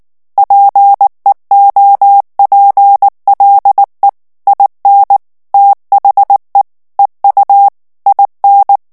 Morse code, heard as a keyed tone, about 19 wpm, 800 Hz. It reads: PEOPLE IN THE EU IN
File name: morse mp3 / ANGLAIS texte clair 2 vitesse 18 mots minute